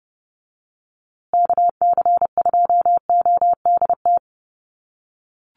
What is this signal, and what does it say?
Morse code, heard as a keyed tone, about 30 wpm, 700 Hz. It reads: KC2ODT
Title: USA Call Signs Repeated 2x 30wpm